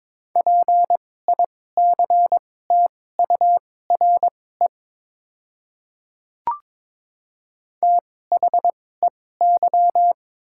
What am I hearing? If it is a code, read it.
Morse code, heard as a keyed tone, about 22 wpm, spaced as for 16 wpm, 700 Hz. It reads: PICTURE THEY
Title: 2x Character Spacing - Top 200 Words - 22wpm